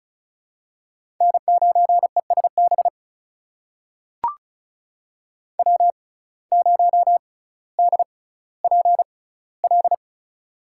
Morse code, heard as a keyed tone, about 34 wpm, spaced as for 18 wpm, 700 Hz. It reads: N9ESB W0DPL
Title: USA Call Signs - 5 Characters - 6x Character Spacing - 35wpm